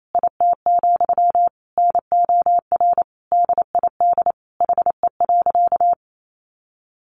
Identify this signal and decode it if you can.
Morse code, heard as a keyed tone, about 28 wpm, 700 Hz. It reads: IT, NOR DID HE.
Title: Sentences from Top 1500 Words 28wpm